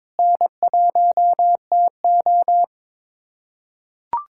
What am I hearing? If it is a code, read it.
Morse code, heard as a keyed tone, about 22 wpm, 700 Hz. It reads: N1TO
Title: K1USN SST QSO Practice - 22wpm